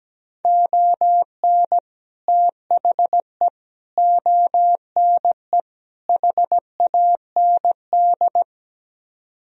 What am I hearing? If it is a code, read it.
Morse code, heard as a keyed tone, about 17 wpm, 700 Hz. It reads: ON THE ONE HAND
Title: Speed-Racing - Sets of 4 Words 17wpm